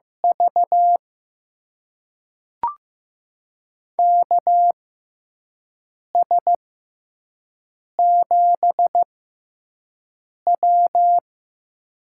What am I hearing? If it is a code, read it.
Morse code, heard as a keyed tone, about 15 wpm, spaced as for 5 wpm, 700 Hz. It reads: V KS7W
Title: USA Call Signs - 4 Characters - 6x Character Spacing - 15wpm